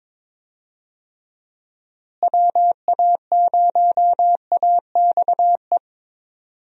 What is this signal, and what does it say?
Morse code, heard as a keyed tone, about 22 wpm, 700 Hz. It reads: WA0AXE
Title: USA Call Signs - 6 Characters - Repeated 2x - 22wpm